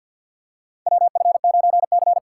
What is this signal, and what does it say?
Morse code, heard as a keyed tone, about 52 wpm, 700 Hz. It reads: WF9C
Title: USA Call Signs - 4 Characters - 2x Character Spacing - 50wpm